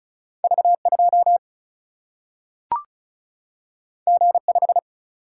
Morse code, heard as a keyed tone, about 36 wpm, 700 Hz. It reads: V2 G5
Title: Single-Letter Single-Number 35wpm